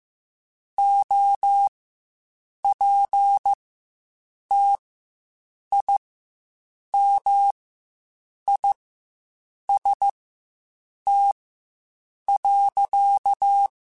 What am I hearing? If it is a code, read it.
Morse code, heard as a keyed tone, about 15 wpm, spaced as for 7 wpm, 800 Hz. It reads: OPTIMIST.
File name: Quotes Spaced